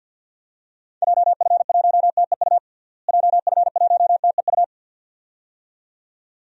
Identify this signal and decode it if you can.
Morse code, heard as a keyed tone, about 51 wpm, 700 Hz. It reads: JF1TEU JF1TEU
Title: DXCC-100 Call Signs - 6 Characters - Repeated 2x - 50wpm